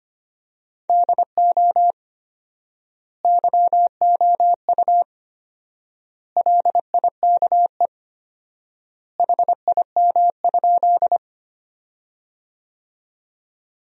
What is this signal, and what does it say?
Morse code, heard as a keyed tone, about 25 wpm, 700 Hz. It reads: DO YOU LIKE HIM?
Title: 4x Word-Spacing - Sentences from Top 100 Words - 25wpm